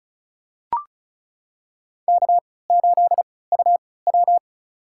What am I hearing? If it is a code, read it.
Morse code, heard as a keyed tone, about 36 wpm, spaced as for 20 wpm, 700 Hz. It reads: K8UW
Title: USA Call Signs - 3x Character Spacing - 35wpm